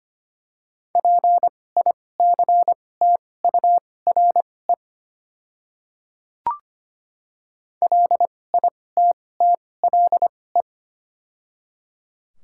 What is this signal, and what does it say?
Morse code, heard as a keyed tone, about 25 wpm, spaced as for 18 wpm, 700 Hz. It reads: PICTURE LITTLE
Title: 2x Character Spacing - Top 200 Words - 25wpm